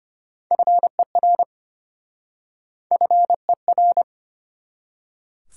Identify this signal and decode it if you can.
Morse code, heard as a keyed tone, about 29 wpm, 700 Hz. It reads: FER FER
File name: practice / 25wpm / Speed-Racing Top 20 QSO Elements - 25wpm